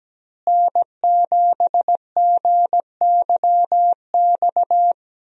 Morse code, heard as a keyed tone, about 17 wpm, 700 Hz. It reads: N7GYX
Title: USA Call Signs - 5 Characters - Repeated 3x - 17wpm